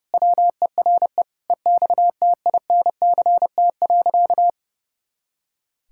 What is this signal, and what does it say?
Morse code, heard as a keyed tone, about 32 wpm, 700 Hz. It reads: WERE EXTINCT.
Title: Pangram Challenge v2 - 30wpm